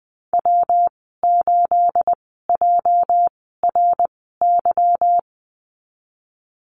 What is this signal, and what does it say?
Morse code, heard as a keyed tone, about 19 wpm, spaced as for 14 wpm, 700 Hz. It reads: W8JRY
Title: 213 Lesson 34g - Number 8 - Call Signs 20wpm